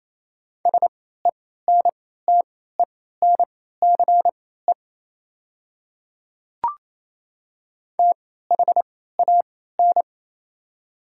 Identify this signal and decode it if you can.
Morse code, heard as a keyed tone, about 28 wpm, spaced as for 16 wpm, 700 Hz. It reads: SENTENCE THAN
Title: 3x Character Spacing - Top 300 Words - 28wpm